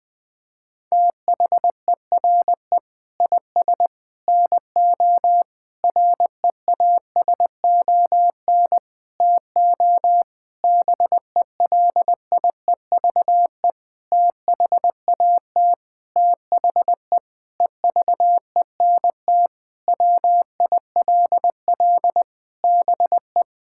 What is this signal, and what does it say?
Morse code, heard as a keyed tone, about 20 wpm, 700 Hz. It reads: THERE IS NO REASON TO BELIEVE THAT THE EVENT WILL BE
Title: Sentences with 8-Word Phrases - Spoken Beginning - 20wpm